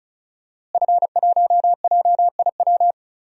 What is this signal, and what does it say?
Morse code, heard as a keyed tone, about 36 wpm, 700 Hz. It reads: F1JIW
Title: DXCC-100 Call Signs - 5 Characters - Repeated 3x - 35wpm